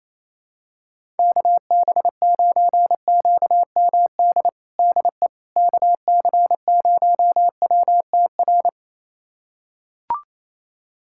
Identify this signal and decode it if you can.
Morse code, heard as a keyed tone, about 28 wpm, 700 Hz. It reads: KB9QMD DE KC0WTR
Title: USA Callsign DE Callsign 28wpm